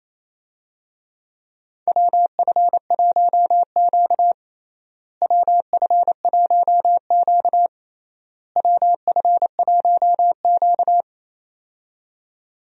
Morse code, heard as a keyed tone, about 28 wpm, 700 Hz. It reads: WF1Q WF1Q WF1Q
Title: USA Call Signs - 4 Characters - Repeated 3x - 28wpm